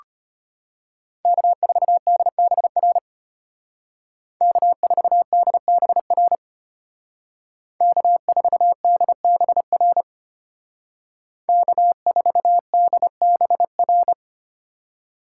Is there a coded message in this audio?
Yes: Morse code from 1.3 to 14.1 s, heard as a keyed tone, about 33 wpm, 700 Hz. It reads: K4DBR K4DBR K4DBR K4DBR